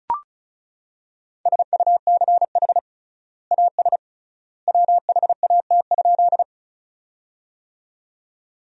Morse code, heard as a keyed tone, about 36 wpm, 700 Hz. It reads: SUCH AS WHAT?
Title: 3x Word-Spacing - Sentences from Top 200 Words - 35wpm